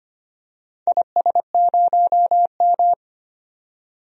Morse code, heard as a keyed tone, about 25 wpm, 700 Hz. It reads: IS0M
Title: DXCC-100 Call Signs - 4 Characters - 25wpm